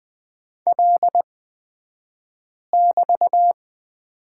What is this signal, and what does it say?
Morse code, heard as a keyed tone, about 20 wpm, spaced as for 5 wpm, 700 Hz. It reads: L<BT>
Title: 223 Lesson 36d - BT - Two Characters 20wpm